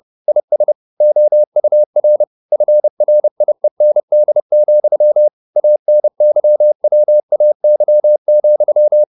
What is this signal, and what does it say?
Morse code, heard as a keyed tone, about 30 wpm, 600 Hz. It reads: IS OUR FRIEND, ANYWAY,